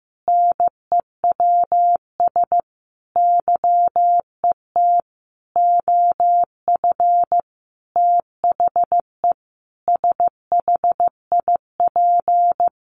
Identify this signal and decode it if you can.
Morse code, heard as a keyed tone, about 15 wpm, 700 Hz. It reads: NEWS YET OF THE SHIP
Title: Sentences from Top 800 Words 15wpm